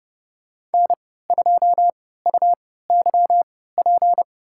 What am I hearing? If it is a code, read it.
Morse code, heard as a keyed tone, about 30 wpm, spaced as for 17 wpm, 690 Hz. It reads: N2UYP
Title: USA Call Signs - 3x Character Spacing - 30wpm